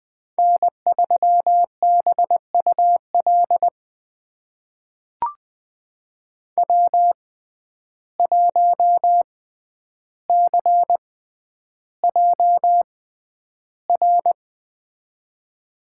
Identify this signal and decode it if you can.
Morse code, heard as a keyed tone, about 20 wpm, spaced as for 10 wpm, 700 Hz. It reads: N3BUL W1CJR
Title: USA Call Signs - 5 Characters - 6x Character Spacing - 20wpm